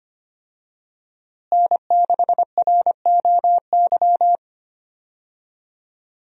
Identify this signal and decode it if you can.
Morse code, heard as a keyed tone, about 25 wpm, 700 Hz. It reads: N6ROY
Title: USA Call Signs - 5 Characters - Repeated 3x - 25wpm